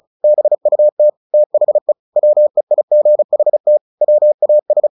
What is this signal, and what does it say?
Morse code, heard as a keyed tone, about 37 wpm, 600 Hz. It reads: BUT THE WEIGHT WAS
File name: the old man and the sea / the old man and the sea 1 35wpm